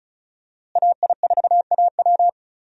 Morse code, heard as a keyed tone, about 36 wpm, 700 Hz. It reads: AI4AW
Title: USA Call Signs - 5 Characters - Repeated 3x - 35wpm